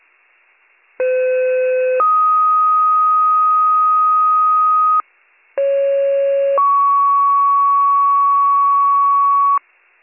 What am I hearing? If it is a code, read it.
Two-tone page, heard as a one-second tone, then a three-second tone, about 510 then 1250 Hz, about 550 then 1120 Hz. It is heard from 1.0 to 9.6 s.